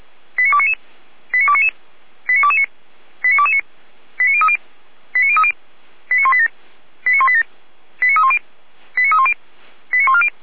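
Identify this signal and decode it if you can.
Five-tone selective call (ZVEI), heard as quick runs of five tones, tones of about 70 ms. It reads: 89299 89299 89229 89229 89030 89030 89187 89187 89310 89310 89140
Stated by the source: ZVEI five-tone selective call received on SDR.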